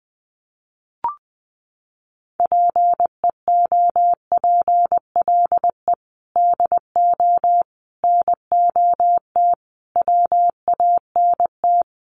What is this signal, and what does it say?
Morse code, heard as a keyed tone, about 20 wpm, 700 Hz. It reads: PEOPLE DO NOT WANT
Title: Sets of 5 Words Encore2 20wpm